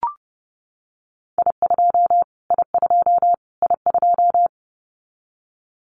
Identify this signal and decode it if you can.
Morse code, heard as a keyed tone, about 30 wpm, 700 Hz. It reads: I2 I2 I2
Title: Callsign Prefixes for Top 25 DXCC - Repeated 3x - 30wpm